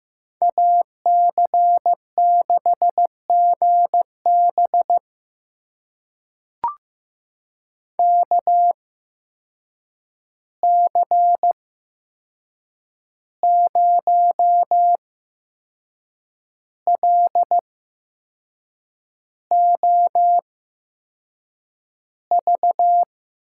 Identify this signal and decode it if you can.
Morse code, heard as a keyed tone, about 15 wpm, spaced as for 6 wpm, 700 Hz. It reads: AC6GB KC0LOV
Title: USA Call Signs - 8x Character Spacing - 15wpm